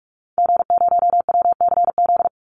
Morse code, heard as a keyed tone, about 44 wpm, 700 Hz. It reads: G0WCZ
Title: Call Signs - 45wpm